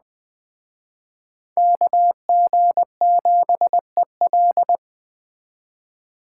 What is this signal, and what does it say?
Morse code, heard as a keyed tone, about 20 wpm, 700 Hz. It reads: KG7EL
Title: Speed Racing - USA Call Signs 17wpm